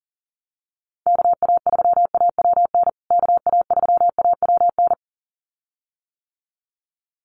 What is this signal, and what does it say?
Morse code, heard as a keyed tone, about 40 wpm, 700 Hz. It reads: KA3AWN KA3AWN
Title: USA Call Signs Repeated 2x 40wpm